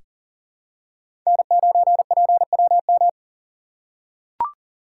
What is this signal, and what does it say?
Morse code, heard as a keyed tone, about 40 wpm, 700 Hz. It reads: N9PWM
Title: USA Call Signs - 3x Character Spacing - 40wpm